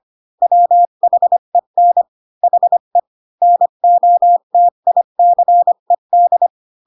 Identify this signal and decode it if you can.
Morse code, heard as a keyed tone, about 25 wpm, 700 Hz. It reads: WHEN HE NOTICED
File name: Pinocchio / Pinocchio ch3 20 25 ik1hge